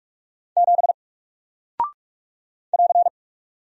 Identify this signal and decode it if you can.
Morse code, heard as a keyed tone, about 44 wpm, 700 Hz. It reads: 7 <AR>
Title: Single Character (Letter:Number:Punctuation:Prosign) 45wpm